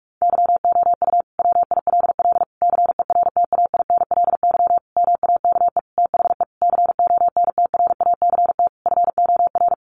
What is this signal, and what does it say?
Morse code, heard as a keyed tone, about 44 wpm, 700 Hz. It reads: YOU WILL CERTAINLY MAKE THE CONTRACT FOR
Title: Sentences from Top 1400 Words 45wpm